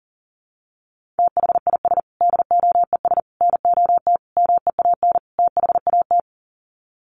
Morse code, heard as a keyed tone, about 40 wpm, 700 Hz. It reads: THIS DOES NOT MEAN THAT
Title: Sets of 5 Words Encore2 40wpm